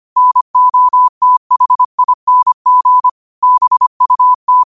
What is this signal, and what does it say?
Morse code, heard as a keyed tone, about 25 wpm, 1000 Hz. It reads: NOTHING BUT